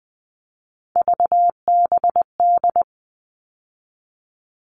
Morse code, heard as a keyed tone, about 20 wpm, 700 Hz. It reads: VBD
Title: All Cut Numbers - 3-Digit 20wpm